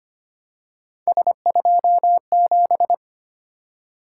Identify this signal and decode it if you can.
Morse code, heard as a keyed tone, about 25 wpm, 700 Hz. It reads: S27
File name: Three Letter-Number Combinations - 25wpm